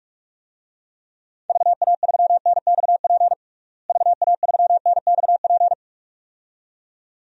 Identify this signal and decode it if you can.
Morse code, heard as a keyed tone, about 45 wpm, 700 Hz. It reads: VA3NXP VA3NXP
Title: DXCC-100 Call Signs - 6 Characters - Repeated 2x - 45wpm